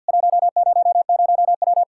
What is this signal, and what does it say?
Morse code, heard as a keyed tone, about 50 wpm, 700 Hz. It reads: 100W
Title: K1USN SST QSO Practice 50wpm